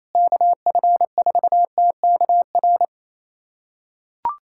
Morse code, heard as a keyed tone, about 28 wpm, 700 Hz. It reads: KF4TKR
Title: USA Callsign DE Callsign - 6 Characters - 28wpm